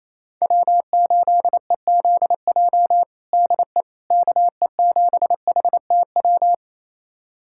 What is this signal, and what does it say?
Morse code, heard as a keyed tone, about 28 wpm, 700 Hz. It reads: W8EZJ DE KE7HTW
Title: USA Callsign DE Callsign 28wpm